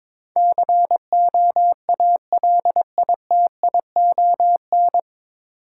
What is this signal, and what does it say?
Morse code, heard as a keyed tone, about 22 wpm, 700 Hz. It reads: COALITION
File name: 9 Letter Words - First 3 Letters Given - 22wpm